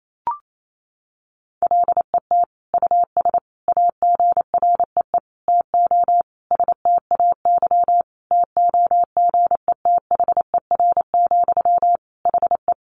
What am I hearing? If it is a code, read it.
Morse code, heard as a keyed tone, about 28 wpm, 700 Hz. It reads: LET US AGREE TO STAY TOGETHER, HE
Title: Sentences from Top 1100 Words 28wpm